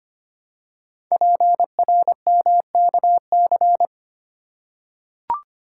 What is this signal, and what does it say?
Morse code, heard as a keyed tone, about 26 wpm, 700 Hz. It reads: PRMKC
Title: Code Groups - 5 Letters - 25wpm